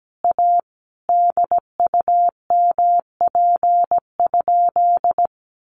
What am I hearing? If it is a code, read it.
Morse code, heard as a keyed tone, about 17 wpm, 700 Hz. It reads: A DUMP?
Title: Sentences from Top 1300 Words 17wpm